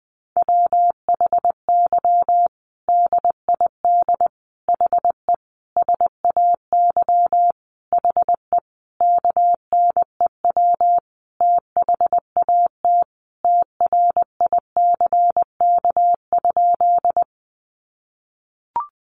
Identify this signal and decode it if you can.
Morse code, heard as a keyed tone, about 20 wpm, 700 Hz. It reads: WHY DID HE SAY HE KNEW THAT TRICK?
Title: Sentences from Top 1500 Words 20wpm